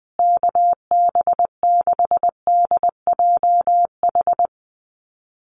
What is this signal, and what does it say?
Morse code, heard as a keyed tone, about 20 wpm, 700 Hz. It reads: KB6DJH
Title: USA Call Signs Repeated 3x 20wpm